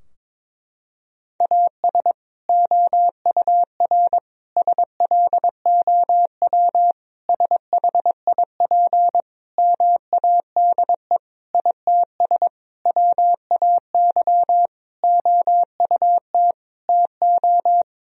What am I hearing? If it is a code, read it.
Morse code, heard as a keyed tone, about 22 wpm, 700 Hz. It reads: AS OUR SLOW SHIP MADE ITS WAY OUT TO
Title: Sentences from Top 1000 Words - Review Part 2 - 22wpm